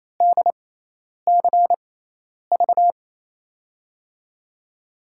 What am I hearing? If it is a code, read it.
Morse code, heard as a keyed tone, about 28 wpm, spaced as for 10 wpm, 700 Hz. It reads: DCV
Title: USA Call Signs - 5 Characters - 6x Character Spacing - 28wpm